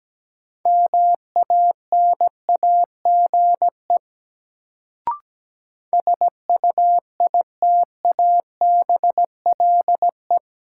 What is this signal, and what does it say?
Morse code, heard as a keyed tone, about 17 wpm, 700 Hz. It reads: MANAGE SUITABLE
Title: Sentences from Top 1400 Words - New Word Drill - 17wpm